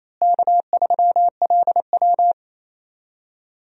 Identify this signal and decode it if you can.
Morse code, heard as a keyed tone, about 28 wpm, 700 Hz. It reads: K3LW
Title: USA Call Signs - 4 Characters - Repeated 3x - 28wpm